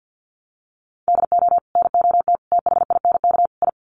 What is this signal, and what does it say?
Morse code, heard as a keyed tone, about 50 wpm, 700 Hz. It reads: DO NOT THINK I
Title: Sets of 4 Words 50wpm